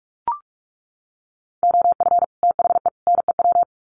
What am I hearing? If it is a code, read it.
Morse code, heard as a keyed tone, about 44 wpm, 700 Hz. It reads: OF THE NEW
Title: Sets of 3 Words 45wpm